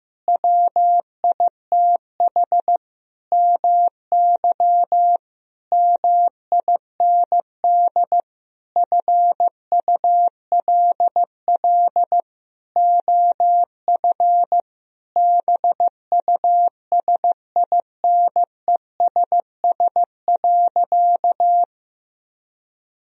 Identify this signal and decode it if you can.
Morse code, heard as a keyed tone, about 15 wpm, 700 Hz. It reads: WITH MY MIND FULL OF BUSINESS.
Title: Sentences from Top 1000 Words - Review Part 3 - 15wpm